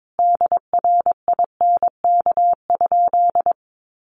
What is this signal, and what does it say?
Morse code, heard as a keyed tone, about 22 wpm, 700 Hz. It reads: DRINK?
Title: Sentences from Top 1500 Words 22wpm